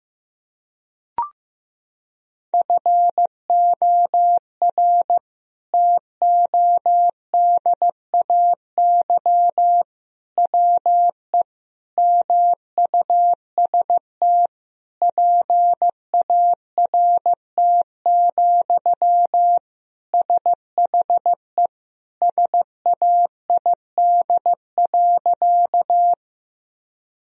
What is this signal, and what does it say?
Morse code, heard as a keyed tone, about 16 wpm, 700 Hz. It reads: FOR TODAY WE MUST PART, SHE SAID.
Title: Sentences from Top 1000 Words - Review Part 1 - 15wpm